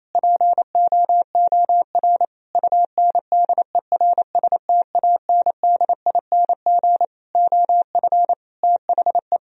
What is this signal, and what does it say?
Morse code, heard as a keyed tone, about 27 wpm, 700 Hz. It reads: POOR UNDERSTANDING OF THE
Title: Sentences from Top 1000 Words - Review Part 3 - 28wpm